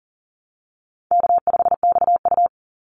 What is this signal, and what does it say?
Morse code, heard as a keyed tone, about 40 wpm, 700 Hz. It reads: K5XU
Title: Easy Call Signs Repeated 3x - 40wpm